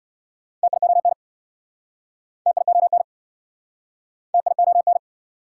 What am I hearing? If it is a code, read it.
Morse code, heard as a keyed tone, about 69 wpm, 690 Hz. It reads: TION TION TION